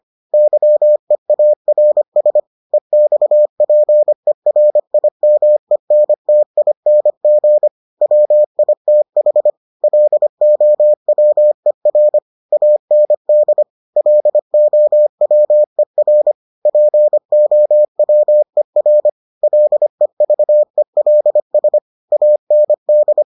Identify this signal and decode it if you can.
Morse code, heard as a keyed tone, about 25 wpm, 600 Hz. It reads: YEARS EXPERIMENTING WITH LOWER AND LOWER POWER LEVELS AND